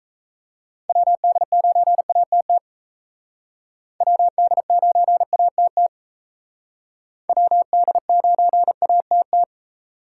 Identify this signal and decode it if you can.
Morse code, heard as a keyed tone, about 36 wpm, 700 Hz. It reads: WD9ATT WD9ATT WD9ATT